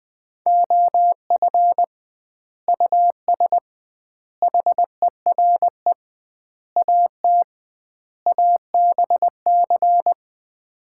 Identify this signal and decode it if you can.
Morse code, heard as a keyed tone, about 20 wpm, 700 Hz. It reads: OF US HERE AT ABC
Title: 2x Word-Spacing - Sets of 5 Words - 20wpm